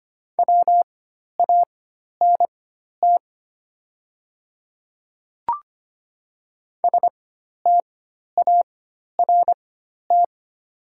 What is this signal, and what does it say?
Morse code, heard as a keyed tone, about 25 wpm, spaced as for 12 wpm, 700 Hz. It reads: WANT START
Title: 4x Character Spacing - Top 300 Words - 25wpm